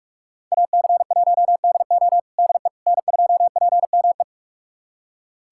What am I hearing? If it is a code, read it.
Morse code, heard as a keyed tone, about 46 wpm, 700 Hz. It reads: AC1DO DE N2PME
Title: USA Callsign DE Callsign - 5 Characters - 45wpm